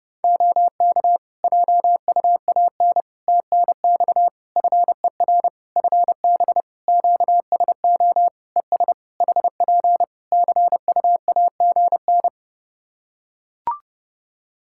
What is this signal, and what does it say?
Morse code, heard as a keyed tone, about 30 wpm, 700 Hz. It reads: OK JUAN TNX FER FB QSO ES HP CUAGN
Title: K1USN SST QSO Practice 30wpm